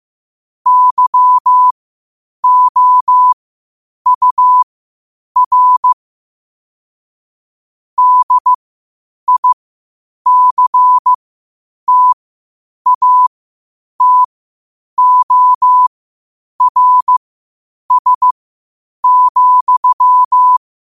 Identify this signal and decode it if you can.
Morse code, heard as a keyed tone, about 15 wpm, spaced as for 8 wpm, 1000 Hz. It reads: YOUR DICTATORS,
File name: 8wpm